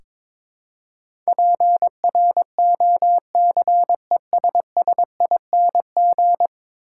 Morse code, heard as a keyed tone, about 22 wpm, 700 Hz. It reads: PROCESSING
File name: practice / 22wpm / 10 Letter Words - First 3 Letters Given - 22wpm